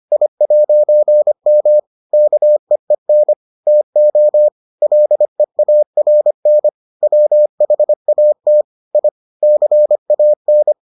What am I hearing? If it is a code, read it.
Morse code, heard as a keyed tone, about 25 wpm, 600 Hz. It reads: I'M KEEN TO LEARN WHAT I CAN